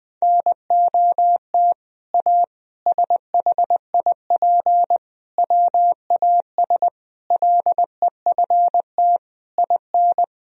Morse code, heard as a keyed tone, about 21 wpm, 700 Hz. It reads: NOT A SHIP WAS LEFT IN
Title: Sentences from Top 1000 Words - Review Part 2 - 20wpm